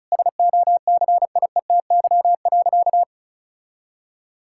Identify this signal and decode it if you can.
Morse code, heard as a keyed tone, about 36 wpm, 690 Hz. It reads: SOCIETY.
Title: Sentences from Top 1600 Words 35wpm